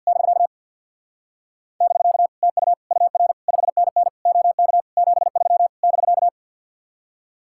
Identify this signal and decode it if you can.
Morse code, heard as a keyed tone, about 50 wpm, 700 Hz. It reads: <BK> <BK> TU UR 5NN OK 73 <BK>
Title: Speed-Racing Common QSO Phrases Encore - 50wpm